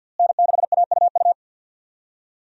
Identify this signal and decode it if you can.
Morse code, heard as a keyed tone, about 52 wpm, 700 Hz. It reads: N6AUU
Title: USA Call Signs - 5 Characters - Repeated 3x - 50wpm